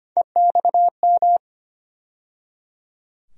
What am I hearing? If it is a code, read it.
Morse code, heard as a keyed tone, about 25 wpm, 700 Hz. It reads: EXM
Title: USA Call Sign Postfixes - 3 Letters 25wpm